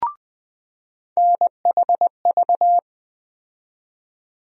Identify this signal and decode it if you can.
Morse code, heard as a keyed tone, about 20 wpm, 700 Hz. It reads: NHV
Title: USA Call Sign Postfixes - 3 Letters 20wpm